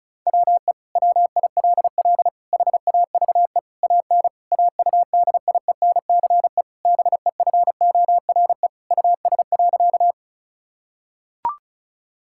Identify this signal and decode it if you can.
Morse code, heard as a keyed tone, about 36 wpm, 700 Hz. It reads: WE WILL HAVE AN AUDIENCE BEFORE US.
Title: Sentences from Top 1000 Words - Review Part 5 - 35wpm